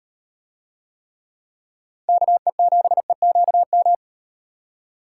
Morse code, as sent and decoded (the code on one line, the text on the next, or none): -.- . --... --.- --
KE7EQM